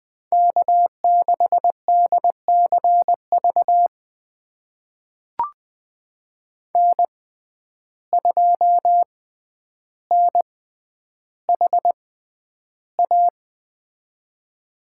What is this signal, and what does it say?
Morse code, heard as a keyed tone, about 20 wpm, spaced as for 10 wpm, 700 Hz. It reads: K6DCV N2NHA
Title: USA Call Signs - 5 Characters - 6x Character Spacing - 20wpm